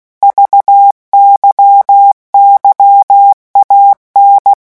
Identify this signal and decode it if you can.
Morse code, heard as a keyed tone, about 16 wpm, 800 Hz. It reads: VYYAN
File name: morse mp3 / revision3-vitesse 15 mots